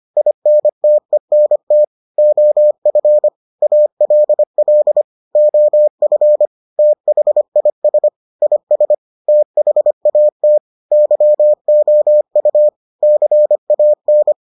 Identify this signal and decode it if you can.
Morse code, heard as a keyed tone, about 25 wpm, 600 Hz. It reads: INTENT OF ALL OF THIS IS THAT YOU CAN